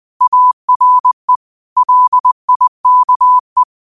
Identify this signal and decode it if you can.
Morse code, heard as a keyed tone, about 20 wpm, 1000 Hz. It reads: ARE LIKE
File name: qotdfiles / 20wpm